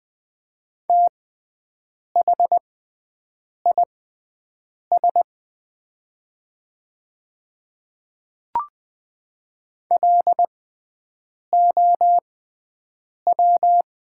Morse code, heard as a keyed tone, about 21 wpm, spaced as for 7 wpm, 700 Hz. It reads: THIS LOW
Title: 6x Character Spacing - Top 300 Words - 20wpm